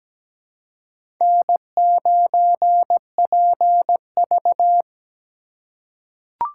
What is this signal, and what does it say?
Morse code, heard as a keyed tone, about 17 wpm, 700 Hz. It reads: N9PV
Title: USA Call Signs - 4 Characters - 3x Character Spacing - 17wpm